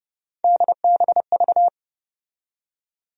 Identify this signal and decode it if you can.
Morse code, heard as a keyed tone, about 30 wpm, 700 Hz. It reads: DBV
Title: USA Call Sign Postfixes - 3 Letters 30wpm